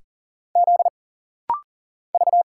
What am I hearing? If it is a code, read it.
Morse code, heard as a keyed tone, about 40 wpm, 700 Hz. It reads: Z V
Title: Single Letters - Rapid-Fire 40wpm